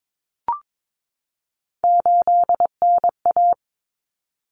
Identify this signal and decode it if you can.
Morse code, heard as a keyed tone, about 22 wpm, 700 Hz. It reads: 8NA
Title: Common Cut Numbers - 3-Digit 22wpm